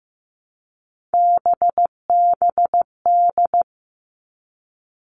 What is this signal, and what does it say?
Morse code, heard as a keyed tone, about 15 wpm, 700 Hz. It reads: BBD
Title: All Cut Numbers - 3-Digit 15wpm